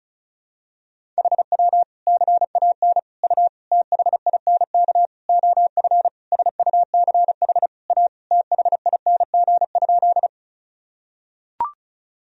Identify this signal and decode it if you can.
Morse code, heard as a keyed tone, about 34 wpm, 700 Hz. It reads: HW CAN U THINK OF SUCH A THING?
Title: Sentences from Top 200 Words and Abbreviations 35wpm